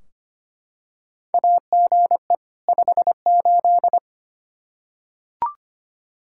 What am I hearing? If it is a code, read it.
Morse code, heard as a keyed tone, about 25 wpm, 700 Hz. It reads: AGE 58
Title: Practice QSO - Exchange 3 - More Info - 25wpm